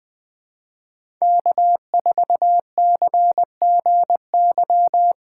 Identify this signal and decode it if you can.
Morse code, heard as a keyed tone, about 20 wpm, 700 Hz. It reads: K4CGY
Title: 2023 POTA Hunter Call Signs - 20wpm